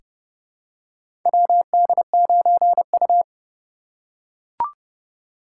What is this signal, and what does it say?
Morse code, heard as a keyed tone, about 30 wpm, 700 Hz. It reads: WD9U
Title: USA Call Signs - 4 Characters - 3x Character Spacing - 30wpm